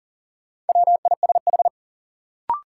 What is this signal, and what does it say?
Morse code, heard as a keyed tone, about 40 wpm, 700 Hz. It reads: WISH
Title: Sentences from Top 1600 Words - New Word Drill Alt - 40wpm